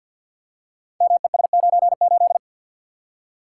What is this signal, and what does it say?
Morse code, heard as a keyed tone, about 52 wpm, 700 Hz. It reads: MES98
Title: Code Groups - 5 Letter:Numbers - 50wpm